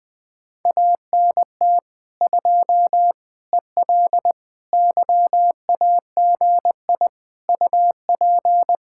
Morse code, heard as a keyed tone, about 20 wpm, 700 Hz. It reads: ANT 2 EL YAGI UP
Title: Speed-Racing Common QSO Phrases - 20wpm